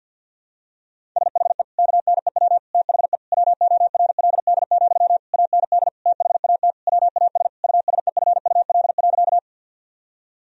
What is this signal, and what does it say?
Morse code, heard as a keyed tone, about 50 wpm, 700 Hz. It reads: SHE KNEW THE WORLD, AND THAT WAS USEFUL.
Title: Sentences from Top 1000 Words - Review Part 4 - 50wpm